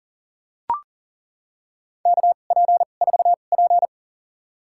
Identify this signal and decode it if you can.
Morse code, heard as a keyed tone, about 40 wpm, spaced as for 29 wpm, 700 Hz. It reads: KP4P
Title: USA Call Signs - 4 Characters - 2x Character Spacing - 40wpm